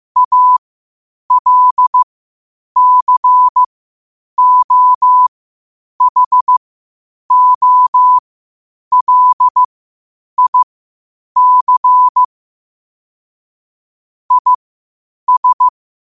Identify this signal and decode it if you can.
Morse code, heard as a keyed tone, about 15 wpm, spaced as for 8 wpm, 1000 Hz. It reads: ALCOHOLIC IS
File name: qotdfiles / 8wpm